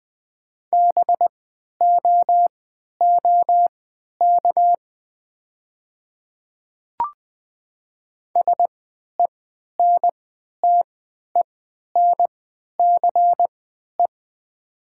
Morse code, heard as a keyed tone, about 20 wpm, spaced as for 11 wpm, 700 Hz. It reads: BOOK SENTENCE
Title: 3x Character Spacing - Top 300 Words - 20wpm